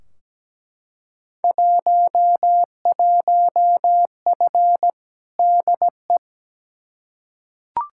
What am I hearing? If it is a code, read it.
Morse code, heard as a keyed tone, about 17 wpm, 700 Hz. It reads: 11F DE
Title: Field Day Exchanges Encore - 17wpm